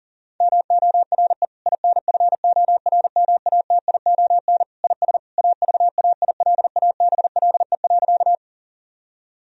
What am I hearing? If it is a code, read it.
Morse code, heard as a keyed tone, about 40 wpm, 700 Hz. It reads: MORE INFORMATION IS AVAILABLE.
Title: Sentences from Top 1000 Words - Review Part 3 - 40wpm